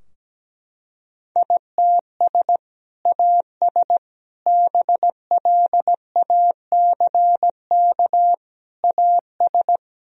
Morse code, heard as a keyed tone, about 17 wpm, 700 Hz. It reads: ITS AS BLACK AS
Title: Sentences from Top 1000 Words - Review Part 2 - 17wpm